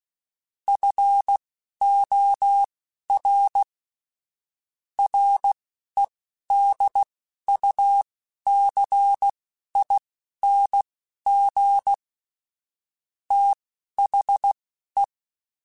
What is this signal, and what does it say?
Morse code, heard as a keyed tone, about 16 wpm, spaced as for 11 wpm, 800 Hz. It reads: FOR REDUCING THE
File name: QSOs / QSOs-mp3s / WD0FBT 1@10